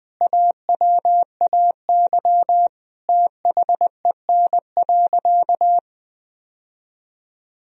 Morse code, heard as a keyed tone, about 20 wpm, 700 Hz. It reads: AWAY THEN.
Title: Sentences from Top 1000 Words - Review Part 4 - 20wpm